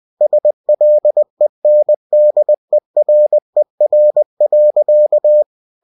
Morse code, heard as a keyed tone, about 20 wpm, 600 Hz. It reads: SLENDERER.
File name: cw-books / winnie-the-pooh / winnie-20wpm-page-0019